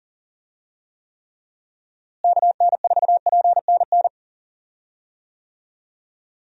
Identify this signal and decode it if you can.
Morse code, heard as a keyed tone, about 39 wpm, 700 Hz. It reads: KN4PNN
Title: USA Call Signs - 6 Characters - Repeated 3x - 40wpm